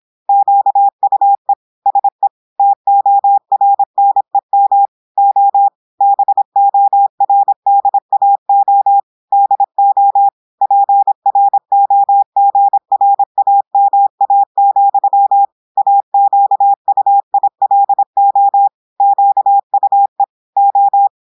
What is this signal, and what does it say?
Morse code, heard as a keyed tone, about 26 wpm, 800 Hz. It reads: QUE SE TORNEM O BORDAO DO PROGRAMA, AQUILO QUE O
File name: portugais texte en clair 2 vitesse 26 mots minutes